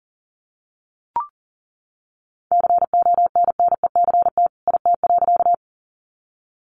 Morse code, heard as a keyed tone, about 40 wpm, 700 Hz. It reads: CONNECT IT.
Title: Sentences from Top 1500 Words 40wpm